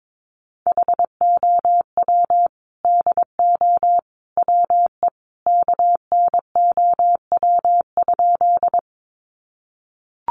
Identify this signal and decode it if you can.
Morse code, heard as a keyed tone, about 22 wpm, 700 Hz. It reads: HOW DO WE KNOW?
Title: A Sound Of Thunder 22wpm